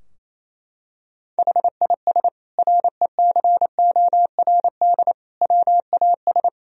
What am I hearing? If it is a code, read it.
Morse code, heard as a keyed tone, about 27 wpm, 690 Hz. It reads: HIS RECORD WAS
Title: Sentences from Top 1000 Words - Review Part 3 - 28wpm